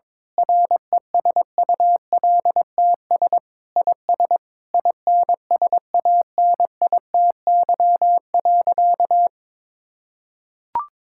Morse code, heard as a keyed tone, about 22 wpm, 700 Hz. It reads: RESULTS IS INSANITY.
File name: Sentences with 9-Word Phrases - 22wpm